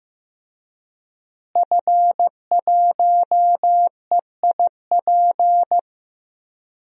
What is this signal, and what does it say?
Morse code, heard as a keyed tone, about 15 wpm, 700 Hz. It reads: F1EIP
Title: DXCC-100 Call Signs - 5 Characters - Repeated 3x - 15wpm